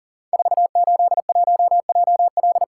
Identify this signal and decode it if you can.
Morse code, heard as a keyed tone, about 39 wpm, 690 Hz. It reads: 481JL
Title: Code Groups - 5 Letter:Numbers - 40wpm